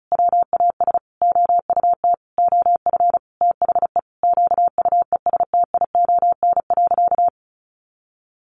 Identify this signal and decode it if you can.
Morse code, heard as a keyed tone, about 34 wpm, 700 Hz. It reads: WAS OUT OF THE QUESTION.
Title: Sentences from Top 600 Words 35wpm